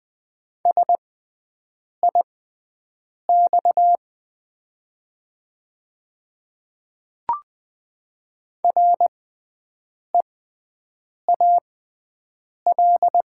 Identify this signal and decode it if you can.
Morse code, heard as a keyed tone, about 20 wpm, spaced as for 7 wpm, 700 Hz. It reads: SIX REAL
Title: 6x Character Spacing - Top 500 Words - 20wpm